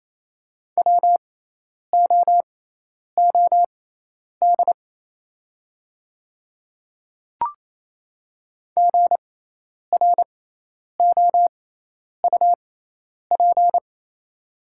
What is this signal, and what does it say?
Morse code, heard as a keyed tone, about 28 wpm, spaced as for 10 wpm, 700 Hz. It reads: WOOD GROUP
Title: 6x Character Spacing - Top 300 Words - 28wpm